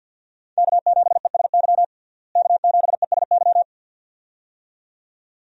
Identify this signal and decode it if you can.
Morse code, heard as a keyed tone, about 47 wpm, 700 Hz. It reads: K7ESY K7ESY
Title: USA Call Signs - 5 Characters - Repeated 2x - 50wpm